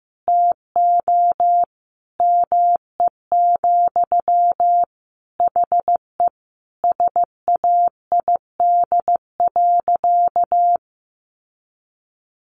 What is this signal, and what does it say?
Morse code, heard as a keyed tone, about 14 wpm, 700 Hz. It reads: TO ME, HE SAID.
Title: Sentences from Top 600 Words 15wpm